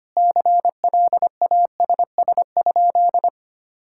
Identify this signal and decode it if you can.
Morse code, heard as a keyed tone, about 25 wpm, 700 Hz. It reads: CLASS?
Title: Sentences from Top 1000 Words - Review Part 6 - 25wpm